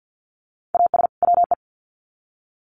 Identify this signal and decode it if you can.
Morse code, heard as a keyed tone, about 50 wpm, 700 Hz. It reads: AS WE
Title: Sets of 2 Words 50wpm